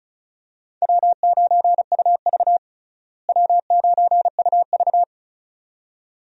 Morse code, heard as a keyed tone, about 36 wpm, 700 Hz. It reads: W9UV W9UV
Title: USA Call Signs - 4 Characters - Repeated 2x - 35wpm